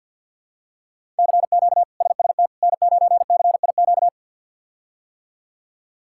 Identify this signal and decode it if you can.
Morse code, heard as a keyed tone, about 50 wpm, 700 Hz. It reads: CQ SST N9CIQ
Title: K1USN SST QSO Practice - 50wpm